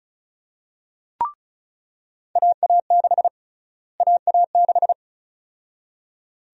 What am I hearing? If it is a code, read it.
Morse code, heard as a keyed tone, about 34 wpm, 700 Hz. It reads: AA6 AA6
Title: USA Call Sign Prefixes - Repeated 2x - 35wpm